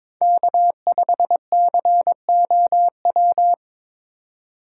Morse code, heard as a keyed tone, about 22 wpm, 700 Hz. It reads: K5COW
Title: USA Call Signs - 5 Characters - 2x Character Spacing - 22wpm